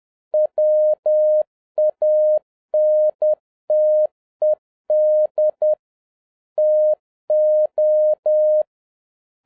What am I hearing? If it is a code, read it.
Morse code, heard as a keyed tone, about 10 wpm, 610 Hz. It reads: WANTED TO